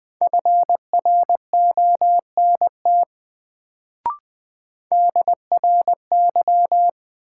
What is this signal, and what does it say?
Morse code, heard as a keyed tone, about 20 wpm, 700 Hz. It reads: FRONT DRY
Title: Sentences from Top 500 Words - New Word Drill - 20wpm